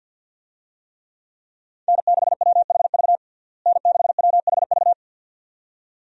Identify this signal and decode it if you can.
Morse code, heard as a keyed tone, about 50 wpm, 700 Hz. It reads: N6WHV N6WHV
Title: USA Call Signs - 5 Characters - Repeated 2x - 50wpm